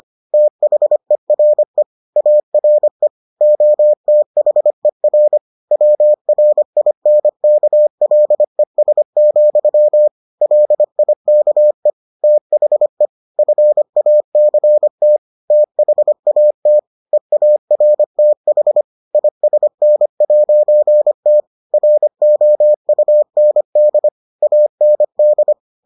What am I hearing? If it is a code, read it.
Morse code, heard as a keyed tone, about 25 wpm, 600 Hz. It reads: THERE ARE OTHER WRINKLES, LIKE THE FACT THAT EARTH ISN'T ROUND AND